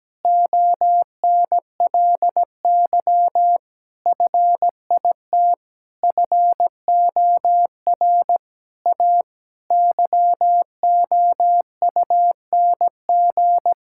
Morse code, heard as a keyed tone, about 17 wpm, 700 Hz. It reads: ONLY FIT FOR A YOUNG
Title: Sentences from Top 1000 Words - Review Part 5 - 17wpm